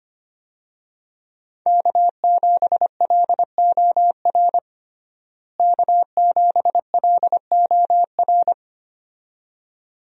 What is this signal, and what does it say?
Morse code, heard as a keyed tone, about 25 wpm, 700 Hz. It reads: K7LOR K7LOR
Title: USA Call Signs - 5 Characters - Repeated 2x - 25wpm